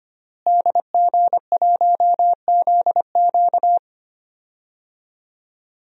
Morse code, heard as a keyed tone, about 25 wpm, 700 Hz. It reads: DG1ZQ
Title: DXCC-100 Call Signs - 5 Characters - Repeated 2x - 25wpm